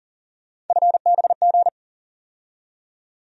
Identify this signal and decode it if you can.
Morse code, heard as a keyed tone, about 40 wpm, 700 Hz. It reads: FBG
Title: USA Call Sign Postfixes - 3 Letters 40wpm